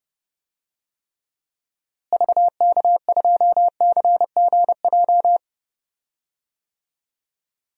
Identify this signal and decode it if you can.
Morse code, heard as a keyed tone, about 30 wpm, 700 Hz. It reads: VK2CGJ
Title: DXCC-100 Call Signs - 6 Characters - Repeated 3x - 30wpm